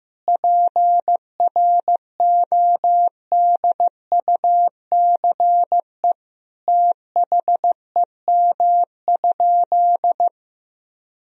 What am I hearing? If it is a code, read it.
Morse code, heard as a keyed tone, about 16 wpm, 700 Hz. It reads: PRODUCE THEM?
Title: Sentences from Top 1000 Words - Review Part 3 - 15wpm